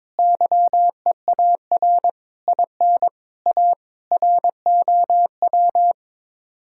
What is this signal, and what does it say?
Morse code, heard as a keyed tone, about 22 wpm, 700 Hz. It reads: YEAR IN A ROW
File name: practice / 22wpm / Sets of 7 Words - 22wpm